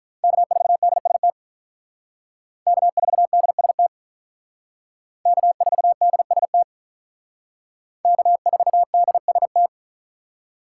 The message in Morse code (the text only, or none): K4DST K4DST K4DST K4DST